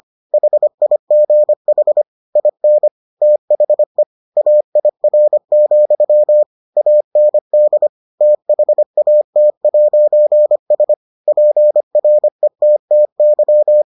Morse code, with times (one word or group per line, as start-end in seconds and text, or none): 0.3-2.0 HIGH
2.4-2.9 IN
3.2-4.0 THE
4.4-6.4 AIR,
6.8-7.9 AND
8.2-10.9 THAT'S
11.3-13.8 PRETTY